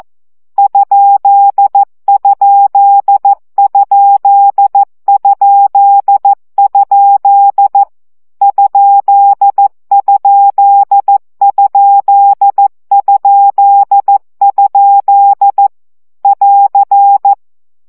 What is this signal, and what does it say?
Morse code, heard as a keyed tone, about 14 wpm, 800 Hz. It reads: ????? ????? <AR>
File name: CW gyakorlas kerdojel 12 baud